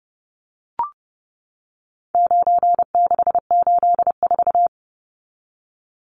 Morse code, heard as a keyed tone, about 30 wpm, 700 Hz. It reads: 9684
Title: Numbers - 4-Digit 30wpm